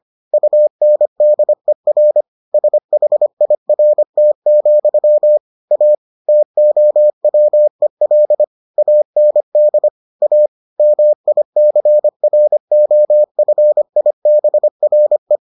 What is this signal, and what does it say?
Morse code, heard as a keyed tone, about 25 wpm, 600 Hz. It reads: UNDER SHIRT, A TOWEL AND A MICROFIBRE